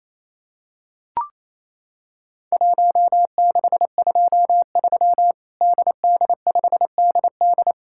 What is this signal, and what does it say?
Morse code, heard as a keyed tone, about 28 wpm, 700 Hz. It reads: 1623 DD5DD
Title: Worked All Europe DX Contest - QTC Exchange - 28wpm